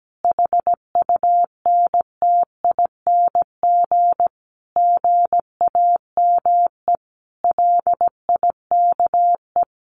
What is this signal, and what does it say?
Morse code, heard as a keyed tone, about 17 wpm, 700 Hz. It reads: HUNTING GAME LIKE
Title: A Sound Of Thunder 17wpm